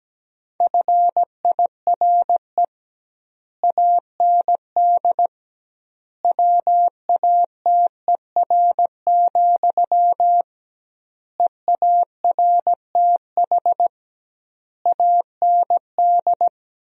Morse code, heard as a keyed tone, about 17 wpm, 700 Hz. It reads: FIRE AND WATER, EARTH AND
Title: 2x Word-Spacing - Sentences from Top 400 Words - 17wpm